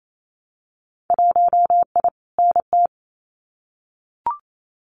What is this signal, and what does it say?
Morse code, heard as a keyed tone, about 28 wpm, 690 Hz. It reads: 1I NT
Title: Winter Field Day Exchanges Encore 28wpm